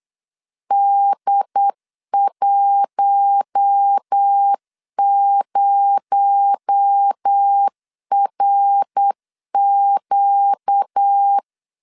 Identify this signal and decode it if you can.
Morse code, heard as a keyed tone, about 8 wpm, 800 Hz. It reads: D10RQ
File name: CW / DET / 1 6 1500C S B 8wpm 5142